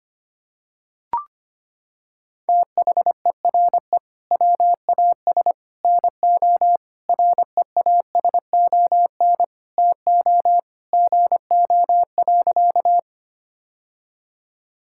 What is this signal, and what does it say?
Morse code, heard as a keyed tone, about 25 wpm, 700 Hz. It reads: THERE WAS NO REASON TO GO.
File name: practice / 25wpm / Sentences with 3-Word Phrases - 25wpm